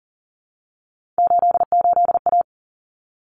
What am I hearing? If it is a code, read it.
Morse code, heard as a keyed tone, about 40 wpm, 700 Hz. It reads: 88A
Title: Common Cut Numbers - 3-Digit 40wpm